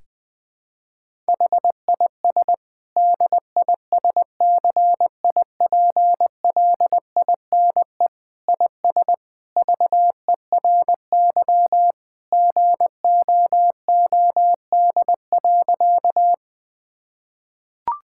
Morse code, heard as a keyed tone, about 20 wpm, 700 Hz. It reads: HIS DISCIPLINE IS VERY GOOD.
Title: Sentences from Top 1000 Words - Review Part 6 - 20wpm